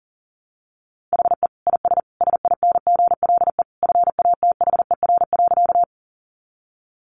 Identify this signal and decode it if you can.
Morse code, heard as a keyed tone, about 40 wpm, 700 Hz. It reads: HE IS SINGLE FATHER.
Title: Sentences from Top 700 Words 40wpm